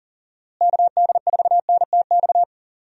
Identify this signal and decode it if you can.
Morse code, heard as a keyed tone, about 40 wpm, 700 Hz. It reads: KD4NTX
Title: USA Call Signs - 6 Characters - Repeated 2x - 40wpm